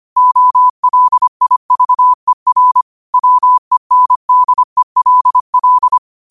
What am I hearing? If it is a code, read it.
Morse code, heard as a keyed tone, about 25 wpm, 1000 Hz. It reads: OLIVER WENDELL
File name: qotdfiles / 25wpm